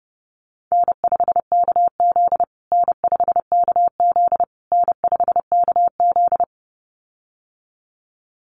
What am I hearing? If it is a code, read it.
Morse code, heard as a keyed tone, about 30 wpm, 700 Hz. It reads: N5KZ N5KZ N5KZ
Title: USA Call Signs Repeated 3x 30wpm